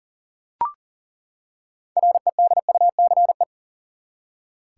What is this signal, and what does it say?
Morse code, heard as a keyed tone, about 40 wpm, 700 Hz. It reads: REDUCE
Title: Six Letter Words - 40wpm